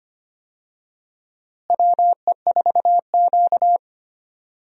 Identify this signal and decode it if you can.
Morse code, heard as a keyed tone, about 25 wpm, 700 Hz. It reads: WE4Q
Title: USA Call Signs - 4 Characters - Repeated 3x - 25wpm